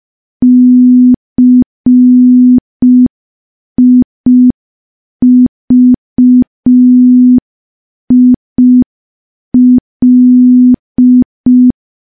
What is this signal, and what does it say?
Morse code, heard as a keyed tone, about 5 wpm, 250 Hz. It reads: CIVIL
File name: morse / morse-training-05WPM